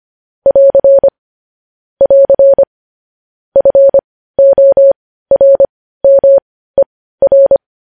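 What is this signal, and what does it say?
Morse code, heard as a keyed tone, about 25 wpm, spaced as for 15 wpm, 550 Hz. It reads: <AR> <AR> FORMER
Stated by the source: Morse code sending of news headlines at 15 wpm.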